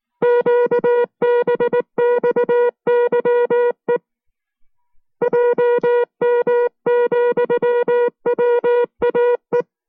Morse code, heard as a keyed tone, about 19 wpm, 480 Hz. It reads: QBXYE JM,WAE